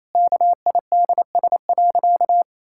Morse code, heard as a keyed tone, about 28 wpm, 700 Hz. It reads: KIDS.
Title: Pangram Challenge v2 - 28wpm